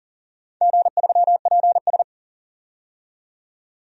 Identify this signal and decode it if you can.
Morse code, heard as a keyed tone, about 40 wpm, 690 Hz. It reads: G3PS
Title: DXCC-100 Call Signs - 4 Characters - 40wpm